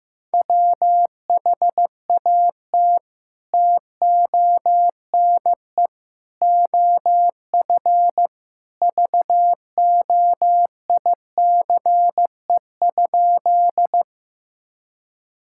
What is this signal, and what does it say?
Morse code, heard as a keyed tone, about 15 wpm, 700 Hz. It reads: WHAT TONE OF VOICE?
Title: Sentences from Top 1600 Words 15wpm